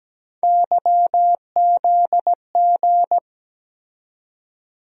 Morse code, heard as a keyed tone, about 17 wpm, 700 Hz. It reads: YZG
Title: USA Call Sign Postfixes - 3 Letters 17wpm